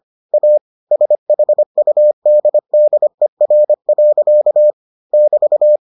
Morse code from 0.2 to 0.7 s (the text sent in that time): A